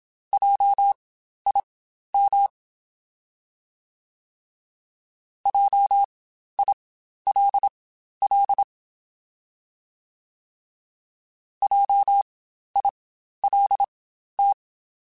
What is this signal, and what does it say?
Morse code, heard as a keyed tone, about 27 wpm, spaced as for 12 wpm, 800 Hz. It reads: JIM JILL JILT
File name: Spaced Alphas / Spaced Alphas mp3s / Js sp @25